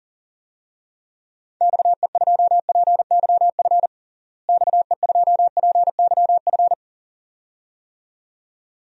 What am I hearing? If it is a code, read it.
Morse code, heard as a keyed tone, about 40 wpm, 700 Hz. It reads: XE2PYF XE2PYF
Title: DXCC-100 Call Signs - 6 Characters - Repeated 2x - 40wpm